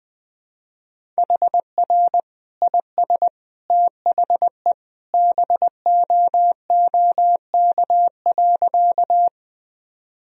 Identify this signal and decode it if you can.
Morse code, heard as a keyed tone, about 20 wpm, 700 Hz. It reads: HR IS THE BOOK.
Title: Sentences from Top 300 Words and Abbreviations 20wpm